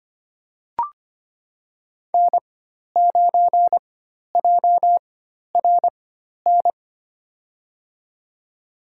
Morse code, heard as a keyed tone, about 25 wpm, spaced as for 12 wpm, 700 Hz. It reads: N9JRN
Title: USA Call Signs - 4x Character Spacing - 25wpm